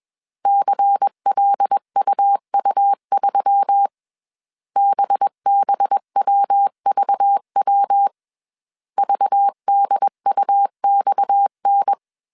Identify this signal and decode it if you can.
Morse code, heard as a keyed tone, about 21 wpm, 800 Hz. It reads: CLUU3 BBWVW VDUXN
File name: CW / DET / 1 6 1500C S A 20wpm 2048